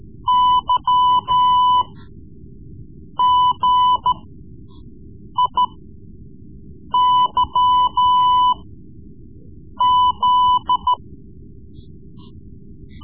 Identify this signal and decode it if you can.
Morse code, heard as a keyed tone, about 13 wpm, spaced as for 5 wpm, 1000 Hz. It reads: YGIYZ
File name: Morse Lessons / Morse Lessons / Tape-3B